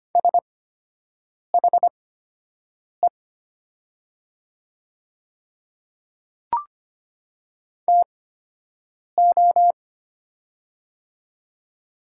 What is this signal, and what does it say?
Morse code, heard as a keyed tone, about 25 wpm, spaced as for 7 wpm, 700 Hz. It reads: SHE TO
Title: 8x Character Spacing - Top 100 Words - 25wpm